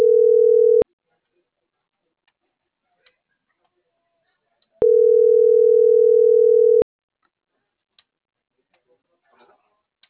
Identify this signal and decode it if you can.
Ringback tone (North America), 440 + 480 Hz.